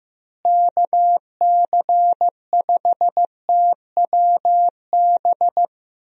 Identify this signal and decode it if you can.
Morse code, heard as a keyed tone, about 15 wpm, 700 Hz. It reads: KC5TWB
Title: USA Call Signs - 6 Characters - Repeated 2x - 15wpm